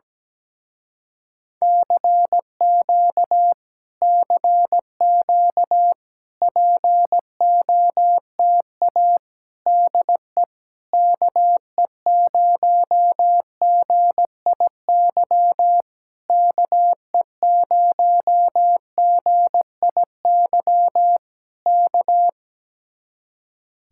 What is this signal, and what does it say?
Morse code, heard as a keyed tone, about 17 wpm, 700 Hz. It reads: CQ CQ POTA DE KE0GIY KE0GIY K
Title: Speed-Racing Common QSO Phrases - 17wpm